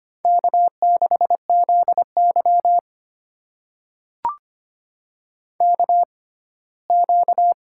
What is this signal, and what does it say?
Morse code, heard as a keyed tone, about 25 wpm, spaced as for 17 wpm, 700 Hz. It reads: K6ZY KQ
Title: USA Call Signs - 4 Characters - 6x Character Spacing - 25wpm